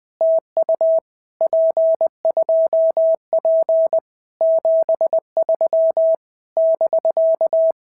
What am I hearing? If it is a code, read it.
Morse code, heard as a keyed tone, about 22 wpm, 650 Hz. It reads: TU P2P 73 <BK>
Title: POTA QSO Practice - 20wpm